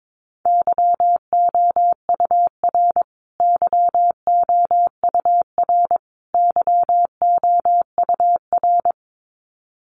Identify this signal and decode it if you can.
Morse code, heard as a keyed tone, about 22 wpm, 700 Hz. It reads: YOUR YOUR YOUR
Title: Top 300 Words Repeated 22wpm